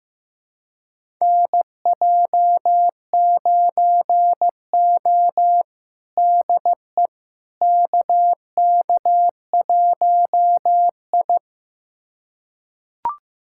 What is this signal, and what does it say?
Morse code, heard as a keyed tone, about 15 wpm, 700 Hz. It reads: NJ9O DE KK1I
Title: USA Callsign DE Callsign - 4 Characters - 15wpm